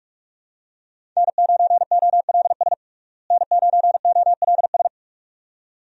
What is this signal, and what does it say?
Morse code, heard as a keyed tone, about 41 wpm, 700 Hz. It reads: N9OLS N9OLS
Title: USA Call Signs - 5 Characters - Repeated 2x - 45wpm